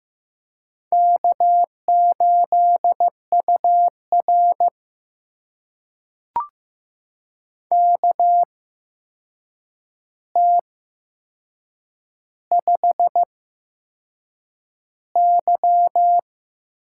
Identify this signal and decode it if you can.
Morse code, heard as a keyed tone, about 15 wpm, spaced as for 6 wpm, 700 Hz. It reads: K8UR KT5Y
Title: USA Call Signs - 4 Characters - 8x Character Spacing - 15wpm